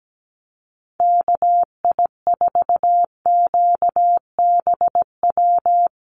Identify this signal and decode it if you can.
Morse code, heard as a keyed tone, about 17 wpm, 700 Hz. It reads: KI4QBW
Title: USA Call Signs Repeated 2x 17wpm